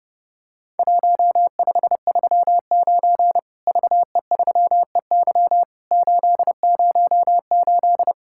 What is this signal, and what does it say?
Morse code, heard as a keyed tone, about 30 wpm, 700 Hz. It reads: 1539 VE3EY 808
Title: Worked All Europe DX Contest - QTC Exchange - 30wpm